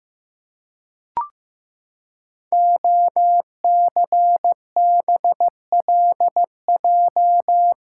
Morse code, heard as a keyed tone, about 15 wpm, 700 Hz. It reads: OCBLJ
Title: Code Groups - 5 Letters - 15wpm